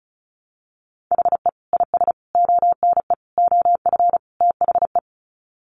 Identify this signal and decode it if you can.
Morse code, heard as a keyed tone, about 34 wpm, 700 Hz. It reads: HE IS ONE OF THE
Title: Sets of 5 Words Encore 35wpm